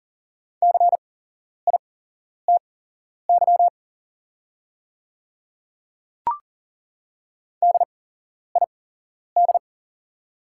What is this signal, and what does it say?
Morse code, heard as a keyed tone, about 40 wpm, spaced as for 11 wpm, 690 Hz. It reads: CITY DID
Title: 8x Character Spacing - Top 500 Words - 40wpm